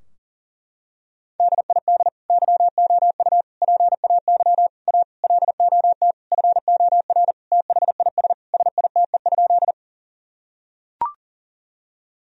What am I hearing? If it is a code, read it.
Morse code, heard as a keyed tone, about 40 wpm, 700 Hz. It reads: DID YOU PAY A LOT FOR THIS SITE?
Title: Sentences from Top 1000 Words - Review Part 4 - 40wpm